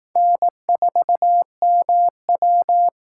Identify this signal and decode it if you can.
Morse code, heard as a keyed tone, about 18 wpm, 700 Hz. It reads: N4MW
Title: Speed Racing - USA 1x2 Call Signs 15wpm